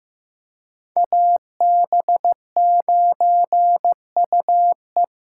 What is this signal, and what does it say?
Morse code, heard as a keyed tone, about 18 wpm, 700 Hz. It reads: AB9UE
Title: Speed Racing - USA 2x2 Call Signs 15wpm